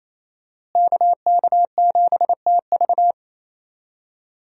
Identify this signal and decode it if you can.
Morse code, heard as a keyed tone, about 27 wpm, 700 Hz. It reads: KK7TV
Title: USA Call Signs - 5 Characters - Repeated 3x - 28wpm